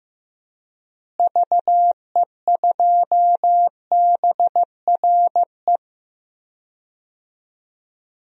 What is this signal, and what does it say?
Morse code, heard as a keyed tone, about 15 wpm, 700 Hz. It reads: VE2BRE
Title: DXCC-100 Call Signs - 6 Characters - Repeated 3x - 15wpm